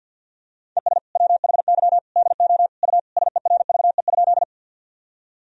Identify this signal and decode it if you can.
Morse code, heard as a keyed tone, about 52 wpm, 700 Hz. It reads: ES WHY DO U SERVE?
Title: Sentences from Top 400 Words and Abbreviations 50wpm